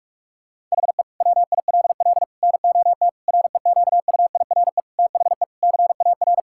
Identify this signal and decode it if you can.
Morse code, heard as a keyed tone, about 41 wpm, 700 Hz. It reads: HE WILL NOT REQUIRE THE CAR
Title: Sentences from Top 1000 Words - Review Part 6 - 45wpm